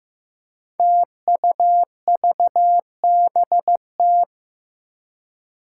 Morse code, heard as a keyed tone, about 15 wpm, 700 Hz. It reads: TUVBT
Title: Five Challenging Letter-Number Combinations 15wpm